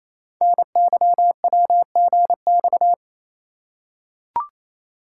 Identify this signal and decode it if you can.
Morse code, heard as a keyed tone, about 28 wpm, 700 Hz. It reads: NYWGX
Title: Code Groups - 5 Letter:Numbers - 28wpm